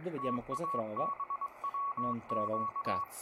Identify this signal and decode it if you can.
Morse code, heard as a keyed tone, about 21 wpm, 1120 Hz, with speech and radio noise. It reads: TABWQ